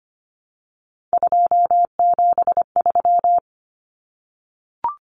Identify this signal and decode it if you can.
Morse code, heard as a keyed tone, about 25 wpm, 690 Hz. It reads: 273
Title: Numbers - 3-Digit 25wpm